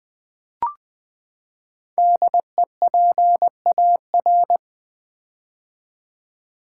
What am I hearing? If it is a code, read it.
Morse code, heard as a keyed tone, about 20 wpm, 700 Hz. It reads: DEPAR
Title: Build A Word - 5 Letters At A Time - 10-Letter Words - 20wpm